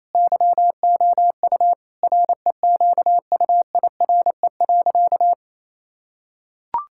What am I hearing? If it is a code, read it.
Morse code, heard as a keyed tone, about 28 wpm, 700 Hz. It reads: YOU REQUIRE.
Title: Sentences from Top 1000 Words - Review Part 6 - 28wpm